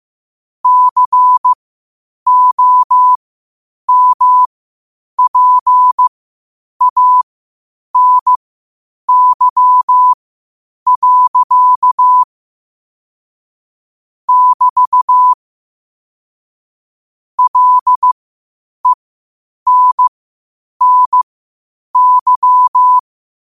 Morse code, heard as a keyed tone, about 15 wpm, spaced as for 8 wpm, 1000 Hz. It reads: COMPANY. <BT> LENNY
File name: 8wpm